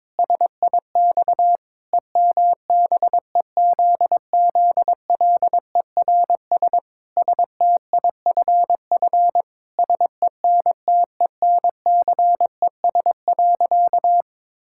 Morse code, heard as a keyed tone, about 22 wpm, 700 Hz. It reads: SIX EMBEZZLERS STIFF SENTENCES.
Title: Pangram Challenge v2 - 22wpm